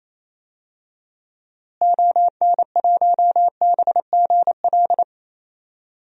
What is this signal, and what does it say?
Morse code, heard as a keyed tone, about 28 wpm, 700 Hz. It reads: ON1BGL
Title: DXCC-100 Call Signs - 6 Characters - Repeated 2x - 28wpm